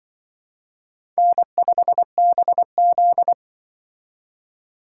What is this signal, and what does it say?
Morse code, heard as a keyed tone, about 25 wpm, 700 Hz. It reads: N5BZ
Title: Speed Racing - USA 1x2 Call Signs 20wpm